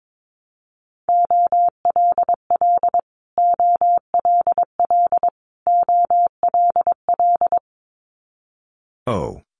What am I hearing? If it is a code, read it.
Morse code, heard as a keyed tone, about 22 wpm, 700 Hz. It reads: OLL OLL OLL
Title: Multi-letter Phonemes Repeated 3x 22wpm